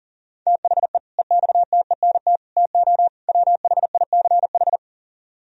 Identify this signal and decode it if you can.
Morse code, heard as a keyed tone, about 53 wpm, 700 Hz. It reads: THE EXTENT TO WHICH